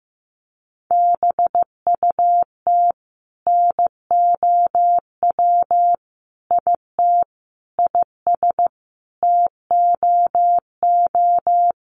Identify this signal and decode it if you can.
Morse code, heard as a keyed tone, about 15 wpm, 700 Hz. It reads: BUT NOW IT IS TOO
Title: Sentences from Top 300 Words 15wpm